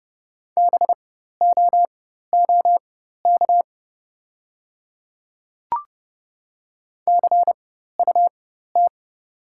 Morse code, heard as a keyed tone, about 30 wpm, spaced as for 14 wpm, 690 Hz. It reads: BOOK CUT
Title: 4x Character Spacing - Top 500 Words - 30wpm